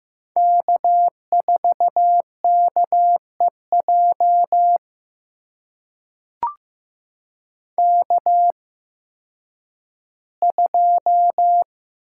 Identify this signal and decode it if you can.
Morse code, heard as a keyed tone, about 15 wpm, spaced as for 10 wpm, 700 Hz. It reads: K4KEJ K2
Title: USA Call Signs - 5 Characters - 8x Character Spacing - 15wpm